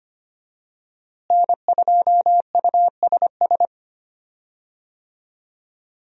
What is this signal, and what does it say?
Morse code, heard as a keyed tone, about 25 wpm, 700 Hz. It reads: N2USS
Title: USA Call Signs - 5 Characters - Repeated 2x - 25wpm